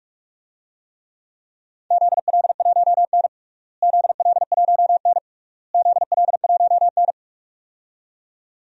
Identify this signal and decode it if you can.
Morse code, heard as a keyed tone, about 41 wpm, 700 Hz. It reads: ZL1N ZL1N ZL1N
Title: DXCC-100 Call Signs - 4 Characters - Repeated 3x - 45wpm